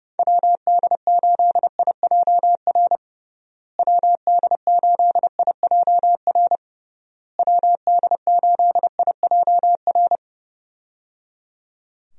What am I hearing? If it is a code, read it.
Morse code, heard as a keyed tone, about 30 wpm, 700 Hz. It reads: WD8IJR WD8IJR WD8IJR
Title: USA Call Signs - 6 Characters - Repeated 3x - 30wpm